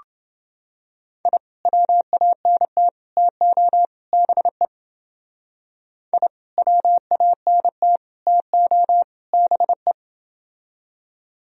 Morse code, heard as a keyed tone, about 29 wpm, 700 Hz. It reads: I WANT TO BE I WANT TO BE